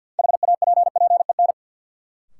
Morse code, heard as a keyed tone, about 52 wpm, 700 Hz. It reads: HAPPEN
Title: Sentences from Top 400 Words - New Word Drill - 50wpm